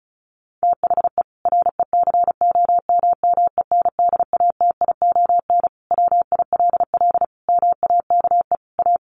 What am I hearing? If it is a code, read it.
Morse code, heard as a keyed tone, about 34 wpm, 700 Hz. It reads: THE RECOMMENDATION WILL MAKE A
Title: Sentences from Top 1400 Words 35wpm